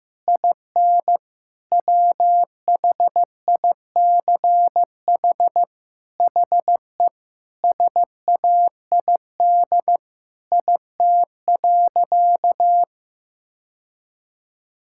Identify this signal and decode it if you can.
Morse code, heard as a keyed tone, about 15 wpm, 700 Hz. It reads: IN WHICH HE SAID IT.
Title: Sentences from Top 1600 Words 15wpm